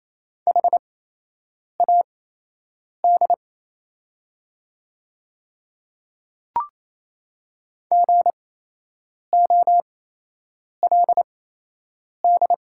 Morse code, heard as a keyed tone, about 27 wpm, spaced as for 8 wpm, 700 Hz. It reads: HAD GOLD
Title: 8x Character Spacing - Top 500 Words - 28wpm